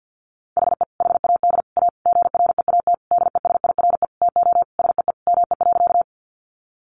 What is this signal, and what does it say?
Morse code, heard as a keyed tone, about 50 wpm, 700 Hz. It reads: HE HAD A GREAT DESIRE TO SEE ME.
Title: Sentences from Top 1100 Words 50wpm